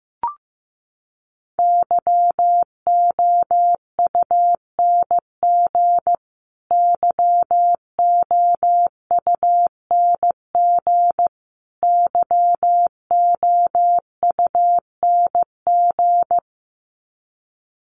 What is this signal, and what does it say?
Morse code, heard as a keyed tone, about 15 wpm, 700 Hz. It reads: YOUNG YOUNG YOUNG
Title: Top 500 Words Repeated 15wpm